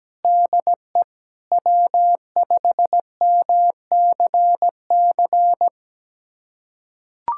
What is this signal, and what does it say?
Morse code, heard as a keyed tone, about 17 wpm, 700 Hz. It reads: DE W5MCC
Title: USA Callsign DE Callsign - 5 Characters - 17wpm